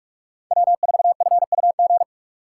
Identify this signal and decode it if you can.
Morse code, heard as a keyed tone, about 42 wpm, 700 Hz. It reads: W4FUG
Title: USA Call Signs - 5 Characters - 2x Character Spacing - 45wpm